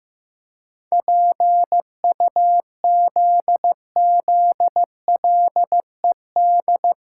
Morse code, heard as a keyed tone, about 16 wpm, 700 Hz. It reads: PUZZLED
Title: Pangram Challenge v2 - 15wpm